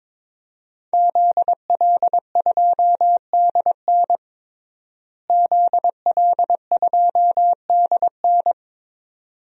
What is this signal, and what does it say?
Morse code, heard as a keyed tone, about 22 wpm, 700 Hz. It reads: ZL2DN ZL2DN
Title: DXCC-100 Call Signs - 5 Characters - Repeated 2x - 22wpm